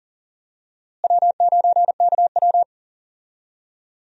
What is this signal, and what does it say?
Morse code, heard as a keyed tone, about 40 wpm, 700 Hz. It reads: W9KW
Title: USA Call Signs - 4 Characters - Repeated 3x - 40wpm